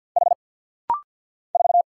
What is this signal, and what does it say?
Morse code, heard as a keyed tone, about 50 wpm, 690 Hz. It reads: H 4
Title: Single Letter-Number - Rapid-Fire 50wpm